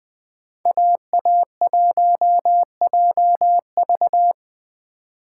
Morse code, heard as a keyed tone, about 20 wpm, 700 Hz. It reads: AA1JV
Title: Speed Racing - USA 2x2 Call Signs 17wpm